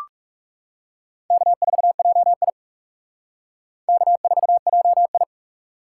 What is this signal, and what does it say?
Morse code, heard as a keyed tone, about 39 wpm, 700 Hz. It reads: K4JI K4JI